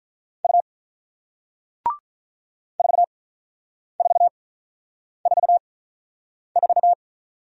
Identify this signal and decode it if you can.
Morse code, heard as a keyed tone, about 47 wpm, spaced as for 9 wpm, 700 Hz. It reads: U 4444